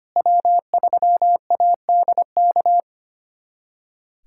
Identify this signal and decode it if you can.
Morse code, heard as a keyed tone, about 25 wpm, 700 Hz. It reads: W3ADK
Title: USA Callsign DE Callsign - 5 Characters - 25wpm